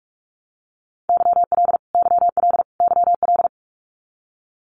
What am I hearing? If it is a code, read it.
Morse code, heard as a keyed tone, about 44 wpm, 700 Hz. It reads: YL YL YL
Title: Abbreviations Repeated 3x - 45wpm